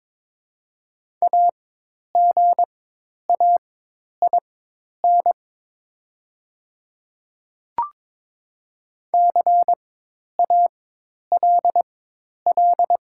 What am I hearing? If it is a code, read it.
Morse code, heard as a keyed tone, about 22 wpm, spaced as for 10 wpm, 700 Hz. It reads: AGAIN CALL
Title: 4x Character Spacing - Top 500 Words - 22wpm